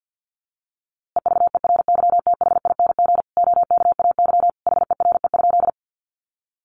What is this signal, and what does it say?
Morse code, heard as a keyed tone, about 50 wpm, 700 Hz. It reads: EVERYTHING OKAY HERE?
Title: A Sound Of Thunder 50wpm